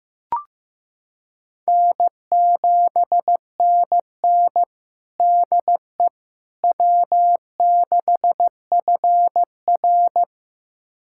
Morse code, heard as a keyed tone, about 15 wpm, 700 Hz. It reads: N7NN DE W6FR
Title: USA Callsign DE Callsign - 4 Characters - 15wpm